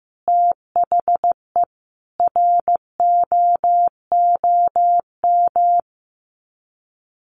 Morse code, heard as a keyed tone, about 15 wpm, 700 Hz. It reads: THE ROOM
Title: Sets of 5 Words Encore 15wpm